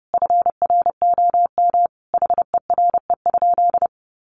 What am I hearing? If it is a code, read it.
Morse code, heard as a keyed tone, about 30 wpm, 700 Hz. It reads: FROM HERE?
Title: Sentences from Top 1100 Words 30wpm